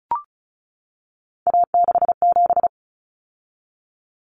Morse code, heard as a keyed tone, about 34 wpm, 700 Hz. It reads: A67
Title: Common Cut Numbers - 3-Digit 35wpm